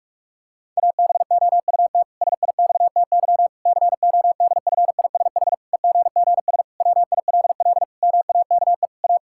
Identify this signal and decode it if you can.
Morse code, heard as a keyed tone, about 46 wpm, 700 Hz. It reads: ABOUT SIXTY CODFISH EGGS WILL MAKE A
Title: Pangram Challenge v2 - 45wpm